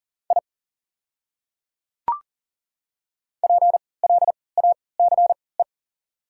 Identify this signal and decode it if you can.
Morse code, heard as a keyed tone, about 39 wpm, spaced as for 23 wpm, 690 Hz. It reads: I PLACE
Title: 3x Character Spacing - Top 200 Words - 40wpm